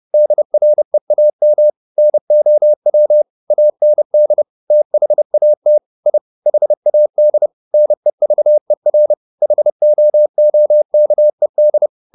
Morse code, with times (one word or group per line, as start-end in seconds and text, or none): 0.1-1.7 DREAM
2.0-3.2 NOW
3.5-4.4 AND
4.7-5.8 THAT
6.1-6.2 I
6.5-7.5 HAD
7.7-9.1 NEVER
9.4-11.9 HOOKED